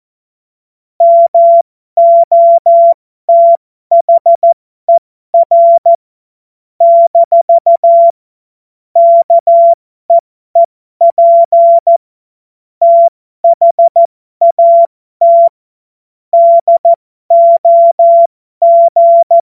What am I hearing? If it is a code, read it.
Morse code, heard as a keyed tone, about 14 wpm, spaced as for 12 wpm, 680 Hz. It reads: MOTHER - KEEP THAT DOG
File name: Morsekurs / Witze